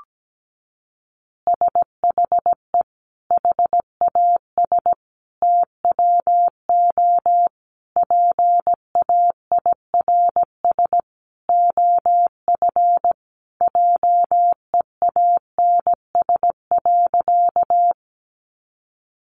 Morse code, heard as a keyed tone, about 17 wpm, 700 Hz. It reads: SHE HAS TWO PAIRS OF JEANS.
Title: Additional Easy Phrases 17wpm